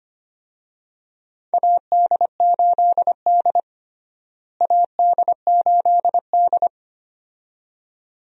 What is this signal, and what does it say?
Morse code, heard as a keyed tone, about 25 wpm, 700 Hz. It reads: AD8D AD8D
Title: USA Call Signs - 4 Characters - Repeated 2x - 25wpm